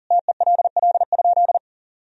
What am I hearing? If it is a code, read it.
Morse code, heard as a keyed tone, about 37 wpm, 700 Hz. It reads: TELL?
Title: Sentences from Top 300 Words and Abbreviations 40wpm